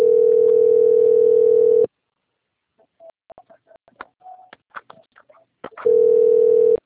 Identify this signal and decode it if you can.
Ringback tone (North America), 440 + 480 Hz.
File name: rg-1600-unknown- - - .81267